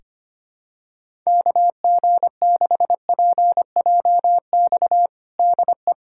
Morse code, heard as a keyed tone, about 25 wpm, 700 Hz. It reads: KG6PJX DE
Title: USA Callsign DE Callsign - 6 Characters - 25wpm